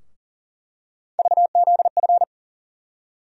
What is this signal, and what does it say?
Morse code, heard as a keyed tone, about 40 wpm, 700 Hz. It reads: VZF
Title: USA Call Sign Postfixes - 3 Letters 40wpm